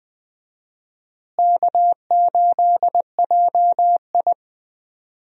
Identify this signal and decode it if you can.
Morse code, heard as a keyed tone, about 21 wpm, 700 Hz. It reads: K8JI
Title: USA Call Signs - 4 Characters - Repeated 3x - 20wpm